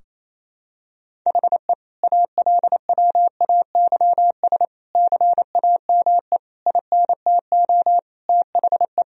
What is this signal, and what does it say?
Morse code, heard as a keyed tone, about 28 wpm, 700 Hz. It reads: HE ALWAYS CAME INTO THE
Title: Sentences from Top 1000 Words - Review Part 4 - 28wpm